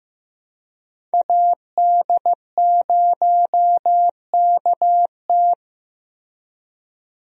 Morse code, heard as a keyed tone, about 15 wpm, 700 Hz. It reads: AD0KT
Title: USA Call Signs - 5 Characters - Repeated 2x - 15wpm